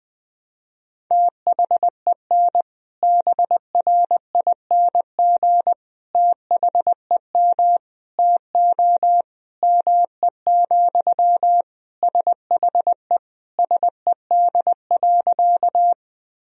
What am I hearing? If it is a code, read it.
Morse code, heard as a keyed tone, about 21 wpm, 700 Hz. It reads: THEN BRING THEM TO ME, SHE SED.
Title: Sentences from Top 500 Words and Abbreviations 20wpm